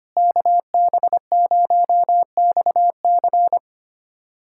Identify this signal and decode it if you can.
Morse code, heard as a keyed tone, about 25 wpm, 700 Hz. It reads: KB0XC
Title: USA Callsign DE Callsign - 5 Characters - 25wpm